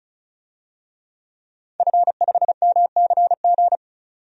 Morse code, heard as a keyed tone, about 34 wpm, 700 Hz. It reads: F5MCG
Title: DXCC-100 Call Signs - 5 Characters - 35wpm